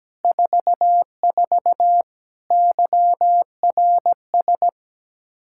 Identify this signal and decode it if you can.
Morse code, heard as a keyed tone, about 17 wpm, 700 Hz. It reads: 44 YRS
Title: Practice QSO - Exchange 3 - More Info - 17wpm